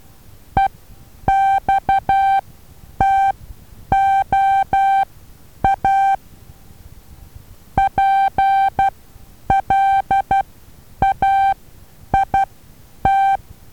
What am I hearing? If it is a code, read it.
Morse code, heard as a keyed tone, about 12 wpm, spaced as for 9 wpm, 780 Hz. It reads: EXTOA PLAIT
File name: Taviro HA6VA B